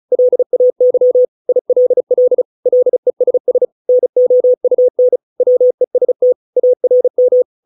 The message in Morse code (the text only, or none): LAY ILL LESS NOUN WEST ARM